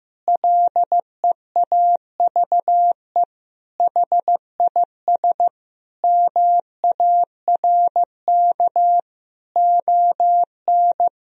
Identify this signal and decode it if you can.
Morse code, heard as a keyed tone, about 16 wpm, 700 Hz. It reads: LEAVE HIS MARK ON
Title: Sentences from Top 1000 Words - Review Part 1 - 15wpm